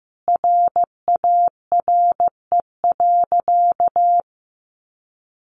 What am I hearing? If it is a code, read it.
Morse code, heard as a keyed tone, about 15 wpm, 700 Hz. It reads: RARE.
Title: Sentences from Top 1500 Words 15wpm